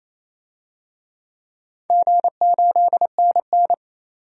Morse code, heard as a keyed tone, about 27 wpm, 700 Hz. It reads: G8NN
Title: DXCC-100 Call Signs - 4 Characters - 28wpm